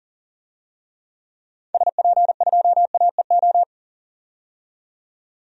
Morse code, heard as a keyed tone, about 40 wpm, 700 Hz. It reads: SP2AEO
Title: DXCC-100 Call Signs - 6 Characters - 40wpm